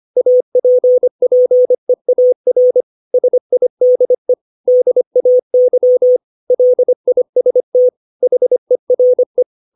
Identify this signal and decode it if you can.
Morse code, heard as a keyed tone, about 25 wpm, 500 Hz. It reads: APPEAR SIDE DAY LIST HERE
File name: 632 words / 632 Words 25WPM